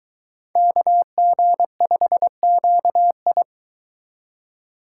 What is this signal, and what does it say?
Morse code, heard as a keyed tone, about 23 wpm, 700 Hz. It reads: KG5QI